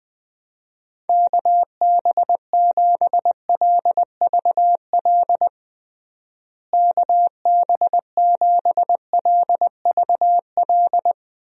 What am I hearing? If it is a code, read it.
Morse code, heard as a keyed tone, about 20 wpm, 700 Hz. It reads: KB7LVL KB7LVL
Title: USA Call Signs - 6 Characters - Repeated 3x - 20wpm